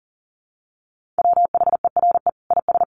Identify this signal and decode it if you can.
Morse code, heard as a keyed tone, about 37 wpm, 700 Hz. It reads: WHERE IS
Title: Sentences from Top 1400 Words 40wpm